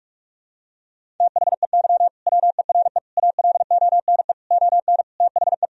Morse code, heard as a keyed tone, about 47 wpm, 700 Hz. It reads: THEY WERE ALONE ON THE
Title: Sentences from Top 1000 Words - Review Part 3 - 45wpm